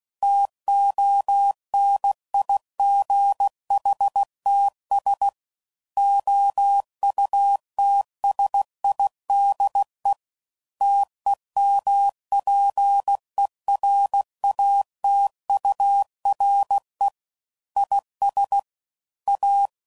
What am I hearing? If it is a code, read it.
Morse code, heard as a keyed tone, about 16 wpm, 800 Hz. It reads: TONIGHTS OUTSIDE TEMPERATURE IS A
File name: CW Code Groups @15WPM / Weather Sets / Weather 4 14